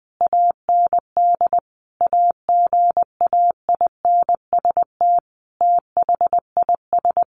Morse code, heard as a keyed tone, about 20 wpm, 700 Hz. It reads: AND AGAINST THIS
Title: Sentences from Top 1300 Words 20wpm